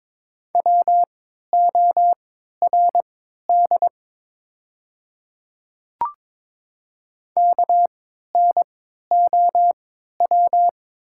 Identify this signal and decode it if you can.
Morse code, heard as a keyed tone, about 22 wpm, spaced as for 13 wpm, 700 Hz. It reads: WORD KNOW
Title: 3x Character Spacing - Top 200 Words - 22wpm